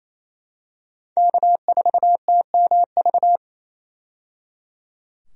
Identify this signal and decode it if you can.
Morse code, heard as a keyed tone, about 28 wpm, 700 Hz. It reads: K4TMV
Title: USA Call Signs - 5 Characters - 28wpm